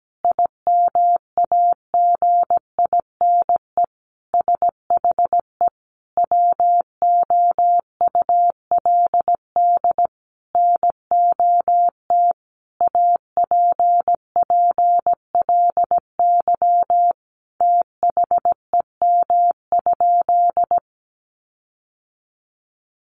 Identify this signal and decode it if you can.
Morse code, heard as a keyed tone, about 17 wpm, 700 Hz. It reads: IMAGINE SHE WOULD NOT APPLY THEM?
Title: Sentences from Top 1400 Words 17wpm